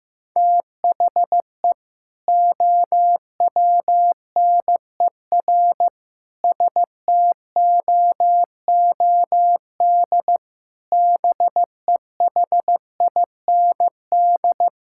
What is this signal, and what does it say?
Morse code, heard as a keyed tone, about 14 wpm, 700 Hz. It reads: THE OWNER STOOD BEHIND
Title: Sentences from Top 1600 Words 15wpm